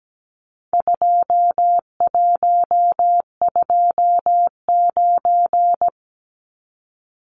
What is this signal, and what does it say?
Morse code, heard as a keyed tone, about 17 wpm, 700 Hz. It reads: 2129
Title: Numbers - 4-Digit 17wpm